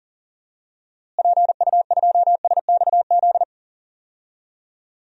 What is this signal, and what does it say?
Morse code, heard as a keyed tone, about 40 wpm, 700 Hz. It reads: PU2SXZ
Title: DXCC-100 Call Signs - 6 Characters - Repeated 2x - 40wpm